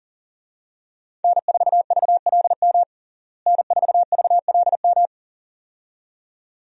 Morse code, heard as a keyed tone, about 40 wpm, 700 Hz. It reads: N4VLM N4VLM
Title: USA Call Signs - 5 Characters - Repeated 2x - 40wpm